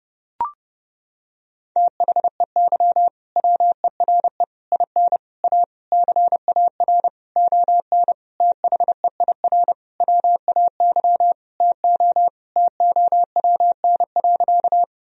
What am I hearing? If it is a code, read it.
Morse code, heard as a keyed tone, about 32 wpm, 700 Hz. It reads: THEY WERE IN A CAR ON THEIR WAY TO TOWN.
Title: Sentences from Top 1000 Words - Review Part 1 - 30wpm